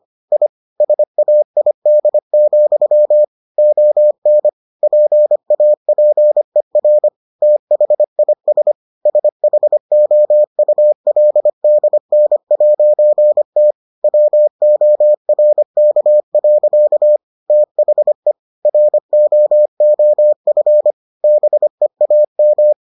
Morse code, heard as a keyed tone, about 25 wpm, 600 Hz. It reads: I SAID, ON PAPER THIS SHOULDN'T WORK. THE ROOF BEAM